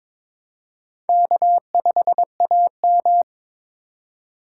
Morse code, heard as a keyed tone, about 21 wpm, 700 Hz. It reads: K5AM
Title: K1USN SST QSO Practice - 22wpm